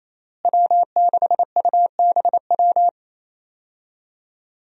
Morse code, heard as a keyed tone, about 27 wpm, 700 Hz. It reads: W6UBW
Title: USA Call Signs - 5 Characters - Repeated 3x - 28wpm